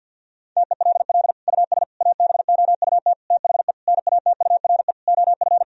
Morse code, heard as a keyed tone, about 50 wpm, 700 Hz. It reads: TELL US ABOUT THE NATURE OF
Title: Sentences from Top 1000 Words - Review Part 4 - 50wpm